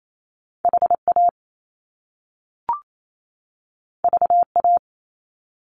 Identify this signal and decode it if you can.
Morse code, heard as a keyed tone, about 28 wpm, 700 Hz. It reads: HA VA
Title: Top Two Letter Combinations in Words 28wpm